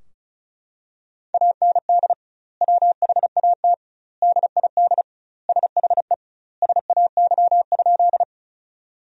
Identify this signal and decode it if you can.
Morse code, heard as a keyed tone, about 34 wpm, 700 Hz. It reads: AND WHAT DID SHE SAY?
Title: 2x Word-Spacing - Sentences from Top 200 Words - 35wpm